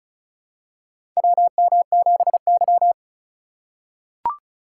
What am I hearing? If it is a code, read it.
Morse code, heard as a keyed tone, about 36 wpm, 700 Hz. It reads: WM7Y
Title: USA Call Signs - 4 Characters - 4x Character Spacing - 35wpm